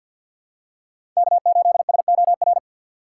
Morse code, heard as a keyed tone, about 51 wpm, 700 Hz. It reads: K8SOR
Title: USA Call Signs - 5 Characters - 3x Character Spacing - 50wpm